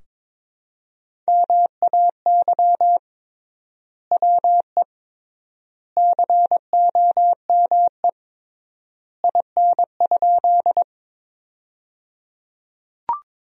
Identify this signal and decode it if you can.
Morse code, heard as a keyed tone, about 22 wpm, 700 Hz. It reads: MAY WE COME IN?
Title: 3x Word-Spacing - Sentences from Top 100 Words - 22wpm